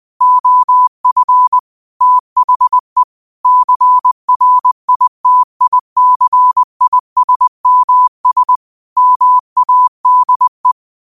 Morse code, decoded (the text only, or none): OF THE CRITICISMS MADE